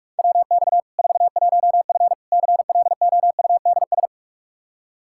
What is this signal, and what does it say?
Morse code, heard as a keyed tone, about 46 wpm, 700 Hz. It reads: WX 41F CLOUDS
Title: Practice QSO - Exchange 2 - Info - 45wpm